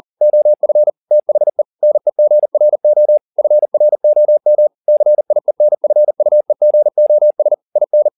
Morse code, heard as a keyed tone, about 40 wpm, 600 Hz. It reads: OF THE NEGRO FROM CIENFUEGOS IN
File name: the old man and the sea 2 40wpm